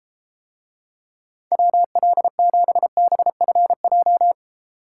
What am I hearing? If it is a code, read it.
Morse code, heard as a keyed tone, about 33 wpm, 700 Hz. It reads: WL7BFJ